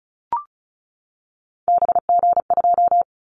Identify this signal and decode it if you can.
Morse code, heard as a keyed tone, about 34 wpm, 700 Hz. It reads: BG2
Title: Callsign Prefixes for Top 100 DXCC - 35wpm